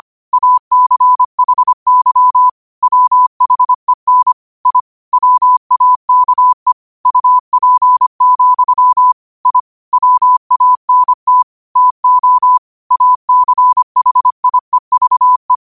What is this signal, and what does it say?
Morse code, heard as a keyed tone, about 25 wpm, 1000 Hz. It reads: ACHY WHEN I WAKE UP, I WANT TO ACHIEVE
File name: qotdfiles / 25wpm